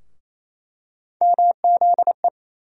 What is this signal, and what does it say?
Morse code, heard as a keyed tone, about 28 wpm, 700 Hz. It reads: MZE
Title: USA Call Sign Postfixes - 3 Letters 28wpm